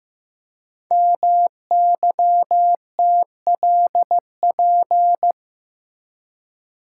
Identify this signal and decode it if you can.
Morse code, heard as a keyed tone, about 15 wpm, 700 Hz. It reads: MYTLP
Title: Code Groups - 5 Letters - 15wpm